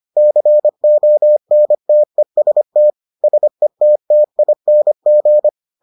Morse code, heard as a keyed tone, about 25 wpm, 600 Hz. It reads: CONTEST SETTING